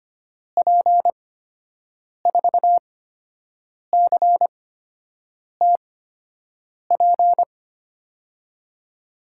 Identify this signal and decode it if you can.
Morse code, heard as a keyed tone, about 25 wpm, spaced as for 7 wpm, 700 Hz. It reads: P4CTP
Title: USA Call Signs - 6 Characters - 8x Character Spacing - 25wpm